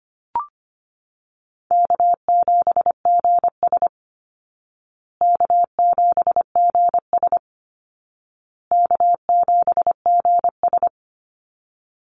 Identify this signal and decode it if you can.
Morse code, heard as a keyed tone, about 25 wpm, 700 Hz. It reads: K7GS K7GS K7GS
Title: Easy Call Signs Repeated 3x - 25wpm